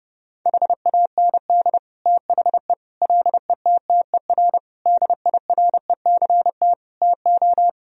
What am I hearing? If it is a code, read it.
Morse code, heard as a keyed tone, about 32 wpm, 700 Hz. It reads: HAND THE LETTER DIRECT TO
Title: Sentences from Top 1000 Words - Review Part 2 - 30wpm